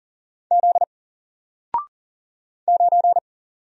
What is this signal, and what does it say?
Morse code, heard as a keyed tone, about 40 wpm, 700 Hz. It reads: Z 9
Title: Single Letter-Number Alt 40wpm